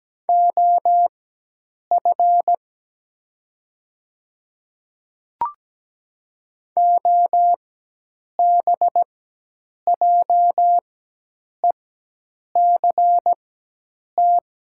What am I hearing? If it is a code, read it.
Morse code, heard as a keyed tone, about 17 wpm, spaced as for 8 wpm, 700 Hz. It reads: OF OBJECT
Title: 4x Character Spacing - Top 500 Words - 17wpm